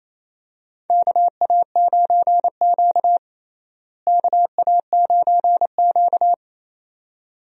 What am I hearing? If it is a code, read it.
Morse code, heard as a keyed tone, about 28 wpm, 700 Hz. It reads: KA9Q KA9Q
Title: USA Call Signs - 4 Characters - Repeated 2x - 28wpm